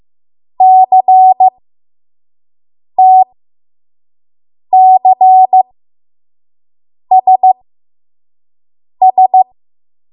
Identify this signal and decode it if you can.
Morse code, heard as a keyed tone, about 15 wpm, spaced as for 5 wpm, 750 Hz. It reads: CTCSS
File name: Morse / 05WPM